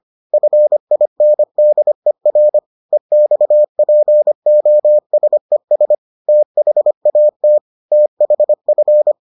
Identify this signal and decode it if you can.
Morse code, heard as a keyed tone, about 25 wpm, 600 Hz. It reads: FINDER EXPOSES THAT THF